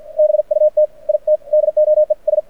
Morse code, heard as a keyed tone, about 50 wpm, spaced as for 44 wpm, 600 Hz. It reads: BUT IT DOES
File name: archived / CRSnet